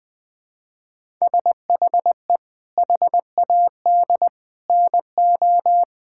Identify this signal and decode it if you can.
Morse code, heard as a keyed tone, about 20 wpm, 700 Hz. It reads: SHE HAD NO
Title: Sentences from Top 1000 Words - Review Part 6 - 20wpm